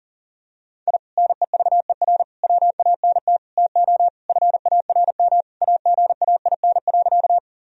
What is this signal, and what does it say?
Morse code, heard as a keyed tone, about 37 wpm, 700 Hz. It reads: I NEVER WANT TO FARM AGAIN.
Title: Sentences from Top 1000 Words - Review Part 2 - 40wpm